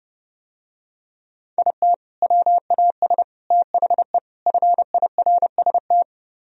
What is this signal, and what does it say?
Morse code, heard as a keyed tone, about 30 wpm, 700 Hz. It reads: IT WAS THE FIRST